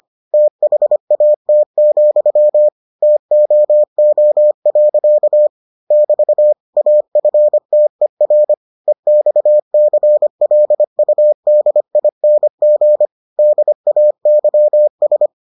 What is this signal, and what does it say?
Morse code, heard as a keyed tone, about 25 wpm, 600 Hz. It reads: THAT, TOO. <BT> AFTER EXCLUDING DAYS